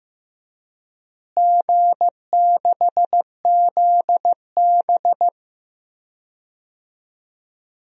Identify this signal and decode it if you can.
Morse code, heard as a keyed tone, about 15 wpm, 700 Hz. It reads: G6ZB
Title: DXCC-100 Call Signs - 4 Characters - Repeated 3x - 15wpm